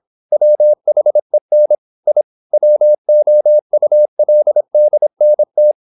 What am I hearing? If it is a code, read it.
Morse code, heard as a keyed tone, about 27 wpm, 600 Hz. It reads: WHEN I WOULDNT
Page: Never 26wpm | CW med Gnister